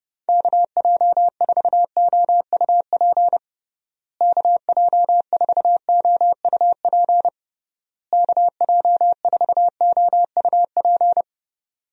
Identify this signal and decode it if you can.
Morse code, heard as a keyed tone, about 30 wpm, 700 Hz. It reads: KJ4OUP KJ4OUP KJ4OUP
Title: USA Call Signs - 6 Characters - Repeated 3x - 30wpm